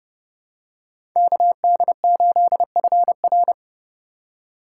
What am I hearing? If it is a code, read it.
Morse code, heard as a keyed tone, about 30 wpm, 700 Hz. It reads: KD8FR